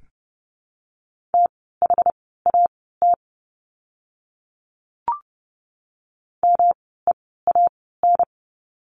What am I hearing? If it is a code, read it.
Morse code, heard as a keyed tone, about 30 wpm, spaced as for 17 wpm, 700 Hz. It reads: THAT MEAN
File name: course / 30:10wpm / 237 Lesson 38f - Letter q - Words 30wpm